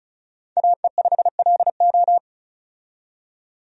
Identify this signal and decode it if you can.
Morse code, heard as a keyed tone, about 36 wpm, 700 Hz. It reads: AE5LO
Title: USA Call Signs - 5 Characters - 35wpm